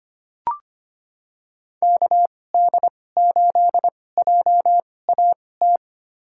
Morse code, heard as a keyed tone, about 25 wpm, spaced as for 18 wpm, 700 Hz. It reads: KD8JAT
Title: USA Call Signs - 6 Characters - 2x Character Spacing - 25wpm